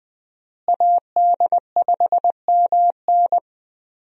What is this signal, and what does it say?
Morse code, heard as a keyed tone, about 20 wpm, 700 Hz. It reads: AD5MN
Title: USA Call Signs - 5 Characters - Repeated 3x - 20wpm